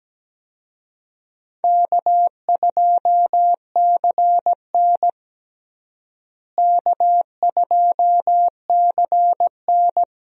Morse code, heard as a keyed tone, about 17 wpm, 700 Hz. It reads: K2CN K2CN
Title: USA Call Signs - 4 Characters - Repeated 2x - 17wpm